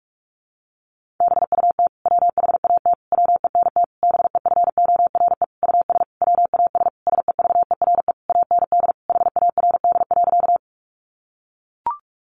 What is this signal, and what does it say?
Morse code, heard as a keyed tone, about 44 wpm, 700 Hz. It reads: BUT WHAT WENT BEFORE US WAS SEVERE AND HARD.
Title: Sentences from Top 1400 Words 45wpm